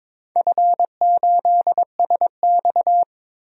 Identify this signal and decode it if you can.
Morse code, heard as a keyed tone, about 21 wpm, 700 Hz. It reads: F8SX
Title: DXCC-100 Call Signs - 4 Characters - Repeated 2x - 22wpm